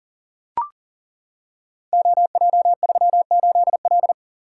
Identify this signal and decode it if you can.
Morse code, heard as a keyed tone, about 43 wpm, 690 Hz. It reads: OJ38L
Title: Code Groups - 5 Letter:Numbers - 40wpm